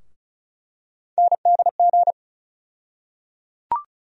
Morse code, heard as a keyed tone, about 36 wpm, 700 Hz. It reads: NDG
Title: USA Call Sign Postfixes - 3 Letters 35wpm